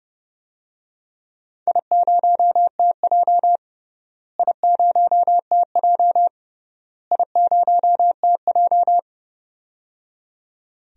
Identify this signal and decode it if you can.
Morse code, heard as a keyed tone, about 30 wpm, 700 Hz. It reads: I0TJ I0TJ I0TJ
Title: DXCC-100 Call Signs - 4 Characters - Repeated 3x - 30wpm